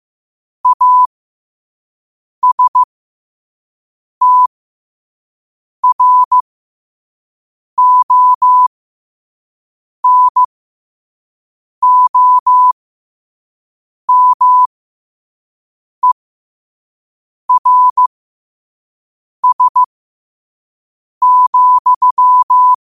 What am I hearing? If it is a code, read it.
Morse code, heard as a keyed tone, about 15 wpm, spaced as for 5 wpm, 1000 Hz. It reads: ASTRONOMERS,
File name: qotdfiles / 5wpm